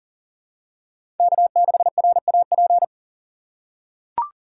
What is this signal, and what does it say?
Morse code, heard as a keyed tone, about 40 wpm, 700 Hz. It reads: K6RAP
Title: USA Call Signs - 5 Characters - 2x Character Spacing - 40wpm